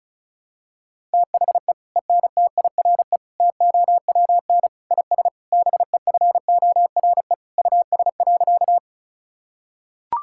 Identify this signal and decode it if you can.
Morse code, heard as a keyed tone, about 34 wpm, 700 Hz. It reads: THE ENTIRE TOWN IS BEFORE US.
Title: Sentences from Top 1000 Words - Review Part 5 - 35wpm